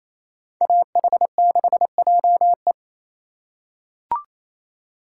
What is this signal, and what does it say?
Morse code, heard as a keyed tone, about 28 wpm, 700 Hz. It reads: AH6JE
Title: USA Call Signs - 5 Characters - 2x Character Spacing - 28wpm